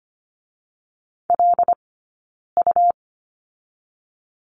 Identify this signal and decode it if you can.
Morse code, heard as a keyed tone, about 25 wpm, 700 Hz. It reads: L U
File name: Lesson 17d - Letter u - Two Characters 25wpm